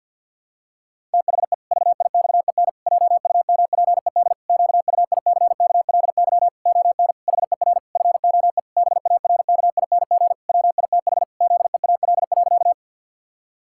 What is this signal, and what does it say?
Morse code, heard as a keyed tone, about 48 wpm, 700 Hz. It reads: THE VIXEN JUMPED QUICKLY ON HER FOE BARKING WITH ZEAL.
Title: Pangram Challenge v2 - 50wpm